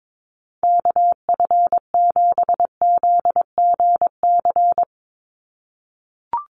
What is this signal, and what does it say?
Morse code, heard as a keyed tone, about 22 wpm, 700 Hz. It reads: KF7ZGC
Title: USA Call Signs 22wpm